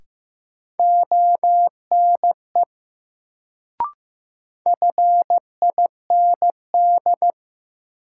Morse code, heard as a keyed tone, about 15 wpm, 700 Hz. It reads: ONE FIND
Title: Sentences from Top 100 Words - New Word Drill - 15wpm